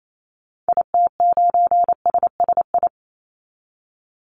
Morse code, heard as a keyed tone, about 28 wpm, 700 Hz. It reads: IT9SSI
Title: Call Signs Repeated 3x - 28wpm